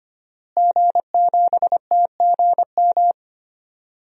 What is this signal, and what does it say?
Morse code, heard as a keyed tone, about 25 wpm, 700 Hz. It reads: G7TGM
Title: DXCC-100 Call Signs - 5 Characters - Repeated 2x - 25wpm